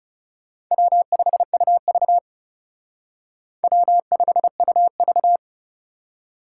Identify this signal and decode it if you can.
Morse code, heard as a keyed tone, about 34 wpm, 700 Hz. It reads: W5UV W5UV
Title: Speed Racing - USA 1x2 Call Signs 30wpm